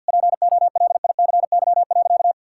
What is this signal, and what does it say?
Morse code, heard as a keyed tone, about 51 wpm, 700 Hz. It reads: POLICY.
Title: Sentences from Top 1000 Words - Review Part 5 - 50wpm